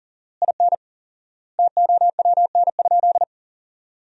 Morse code, heard as a keyed tone, about 40 wpm, 700 Hz. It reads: IN TOWN?
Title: 4x Word-Spacing - Sentences from Top 500 Words - 40wpm